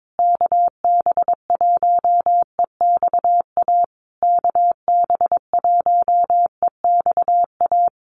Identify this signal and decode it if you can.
Morse code, heard as a keyed tone, about 22 wpm, 700 Hz. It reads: KB1EXA KB1EXA
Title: USA Call Signs Repeated 3x 22wpm